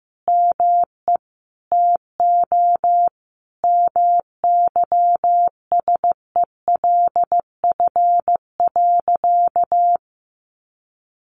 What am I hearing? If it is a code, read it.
Morse code, heard as a keyed tone, about 15 wpm, 700 Hz. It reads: ME TO MYSELF.
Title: Sentences from Top 1300 Words 15wpm